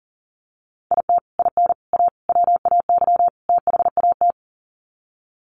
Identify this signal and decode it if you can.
Morse code, heard as a keyed tone, about 40 wpm, 700 Hz. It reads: IT IN A WAY THAT
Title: Sets of 5 Words Encore2 40wpm